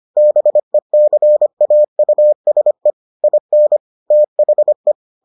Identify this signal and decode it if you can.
Morse code, heard as a keyed tone, about 25 wpm, 600 Hz. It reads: BECAUSE IN THE